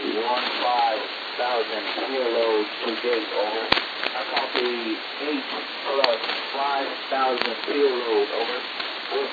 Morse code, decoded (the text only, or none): none